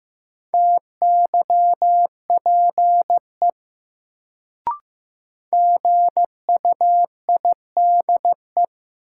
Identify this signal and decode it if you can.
Morse code, heard as a keyed tone, about 15 wpm, 700 Hz. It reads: TYPE GUIDE
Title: Sentences from Top 600 Words - New Word Drill - 15wpm